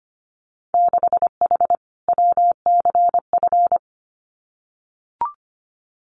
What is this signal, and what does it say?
Morse code, heard as a keyed tone, about 25 wpm, 700 Hz. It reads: 6H WCF
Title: Winter Field Day Exchanges Encore 25wpm